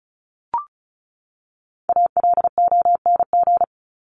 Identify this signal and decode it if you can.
Morse code, heard as a keyed tone, about 34 wpm, 700 Hz. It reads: ALONG
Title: Top 501-750 Words 35wpm